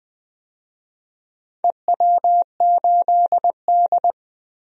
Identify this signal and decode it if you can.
Morse code, heard as a keyed tone, about 20 wpm, 700 Hz. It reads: EW8D
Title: DXCC-100 Call Signs - 4 Characters - 20wpm